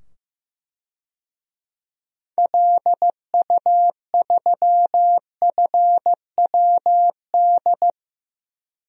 Morse code, heard as a keyed tone, about 16 wpm, 700 Hz. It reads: LU3FWD
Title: DXCC-100 Call Signs - 6 Characters - 15wpm